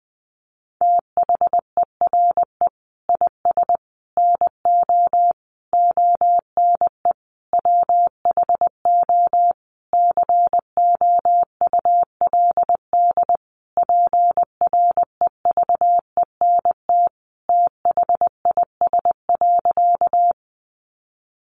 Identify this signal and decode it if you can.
Morse code, heard as a keyed tone, about 20 wpm, 700 Hz. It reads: THERE IS NO ONE WHO COULD PREVENT THIS.
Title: Sentences from Top 1100 Words 20wpm